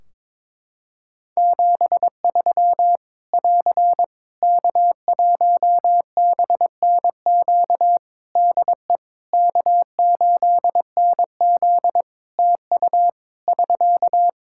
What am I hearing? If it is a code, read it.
Morse code, heard as a keyed tone, about 22 wpm, 700 Hz. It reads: 73 <AR> K1BNQ DE K8NZ TU <SK>
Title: Practice QSO - Exchange 4 - Ending - 22wpm